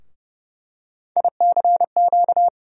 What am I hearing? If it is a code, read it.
Morse code, heard as a keyed tone, about 29 wpm, 700 Hz. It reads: ICQ
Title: USA Call Sign Postfixes - 3 Letters 30wpm